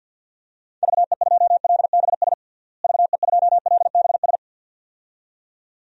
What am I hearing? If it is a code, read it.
Morse code, heard as a keyed tone, about 52 wpm, 700 Hz. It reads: VE2LBS VE2LBS
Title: DXCC-100 Call Signs - 6 Characters - Repeated 2x - 50wpm